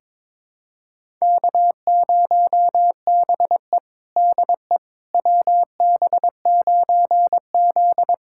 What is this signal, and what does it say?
Morse code, heard as a keyed tone, about 22 wpm, 700 Hz. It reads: K0BE DE WB9Z
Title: USA Callsign DE Callsign - 4 Characters - 22wpm